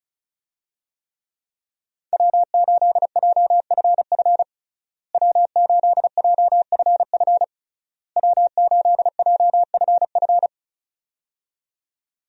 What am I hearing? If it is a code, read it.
Morse code, heard as a keyed tone, about 34 wpm, 700 Hz. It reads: W8JFF W8JFF W8JFF
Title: USA Call Signs - 5 Characters - Repeated 3x - 35wpm